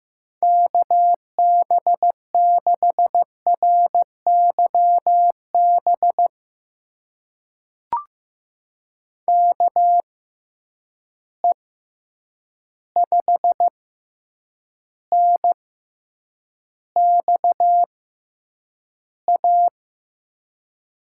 Morse code, heard as a keyed tone, about 15 wpm, spaced as for 8 wpm, 700 Hz. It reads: KB6RYB KE5NXA
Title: USA Call Signs - 6 Characters - 6x Character Spacing - 15wpm